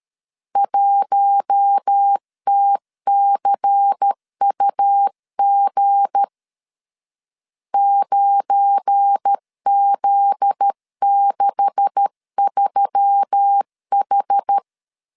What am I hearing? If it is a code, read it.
Morse code, heard as a keyed tone, about 13 wpm, 800 Hz. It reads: 1TCUG 9Z63H
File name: LEZIONI / 9 286 12wpm 2305